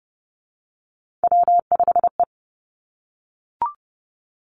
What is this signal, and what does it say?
Morse code, heard as a keyed tone, about 30 wpm, 700 Hz. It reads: W5E
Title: Three Challenging Letter-Number Combinations 30wpm